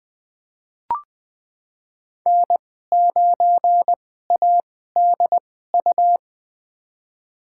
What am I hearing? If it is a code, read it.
Morse code, heard as a keyed tone, about 20 wpm, spaced as for 14 wpm, 700 Hz. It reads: N9ADU
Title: USA Call Signs - 5 Characters - 2x Character Spacing - 20wpm